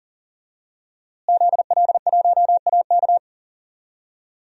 Morse code, heard as a keyed tone, about 40 wpm, 700 Hz. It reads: ZL1AK
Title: DXCC-100 Call Signs - 5 Characters - Repeated 3x - 40wpm